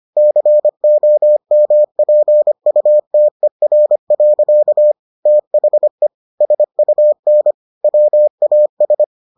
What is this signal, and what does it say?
Morse code, heard as a keyed tone, about 25 wpm, 600 Hz. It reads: COMPUTER. THE SUN WAS